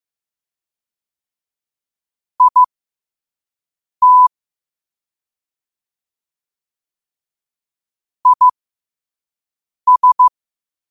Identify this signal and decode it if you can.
Morse code, heard as a keyed tone, about 15 wpm, spaced as for 5 wpm, 1000 Hz. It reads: IT IS
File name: qotdfiles / 5wpm